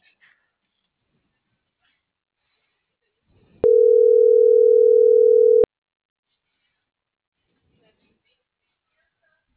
Ringback tone (North America), 440 + 480 Hz.